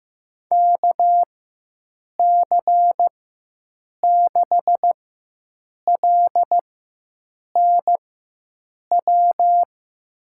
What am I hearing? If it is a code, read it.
Morse code, heard as a keyed tone, about 15 wpm, spaced as for 7 wpm, 700 Hz. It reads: KC6LNW
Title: USA Call Signs - 6 Characters - 4x Character Spacing - 15wpm